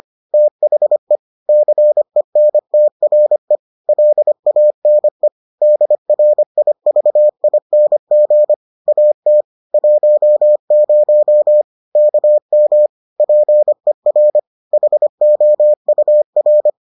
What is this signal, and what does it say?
Morse code, heard as a keyed tone, about 25 wpm, 600 Hz. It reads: THE CENTRE LANE DRIVING AT 10 KM PER HOUR